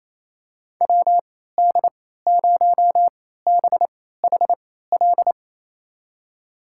Morse code, heard as a keyed tone, about 28 wpm, spaced as for 16 wpm, 700 Hz. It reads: WD0BHL
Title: USA Call Signs - 6 Characters - 3x Character Spacing - 28wpm